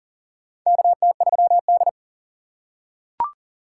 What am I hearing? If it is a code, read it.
Morse code, heard as a keyed tone, about 40 wpm, 700 Hz. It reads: KT3D
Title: USA Call Signs - 4x Character Spacing - 40wpm